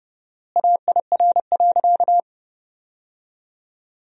Morse code, heard as a keyed tone, about 30 wpm, 700 Hz. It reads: AIR.
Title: 4x Word-Spacing - Sentences from Top 400 Words - 30wpm